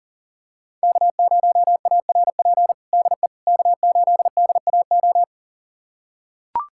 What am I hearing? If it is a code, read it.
Morse code, heard as a keyed tone, about 40 wpm, 700 Hz. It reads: K0ARP DE K8DAO
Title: USA Callsign DE Callsign - 5 Characters - 40wpm